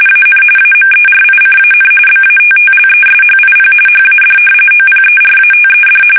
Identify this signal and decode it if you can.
RTTY, about 45.45 baud. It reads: AN AMERICAN BATTLESHIP HAD OPERATED
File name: fsk850 / battleshipnj